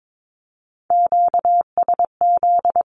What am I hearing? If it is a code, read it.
Morse code, heard as a keyed tone, about 22 wpm, 700 Hz. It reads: QSZ
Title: Q-Codes Repeated 3x 22wpm